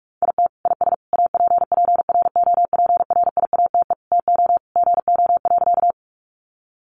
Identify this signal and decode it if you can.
Morse code, heard as a keyed tone, about 44 wpm, 700 Hz. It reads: IT IS APPROPRIATE TO GO.
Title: Sentences from Top 900 Words 45wpm